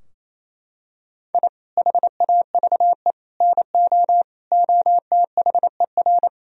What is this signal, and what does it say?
Morse code, heard as a keyed tone, about 28 wpm, 700 Hz. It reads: I HAVE NO OTHER
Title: Sentences from Top 1000 Words - Review Part 4 - 28wpm